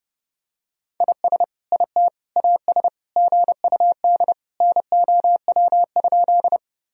Morse code, heard as a keyed tone, about 30 wpm, 700 Hz. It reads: IS IT AS GUD NOW?
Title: Sentences from Top 200 Words and Abbreviations 30wpm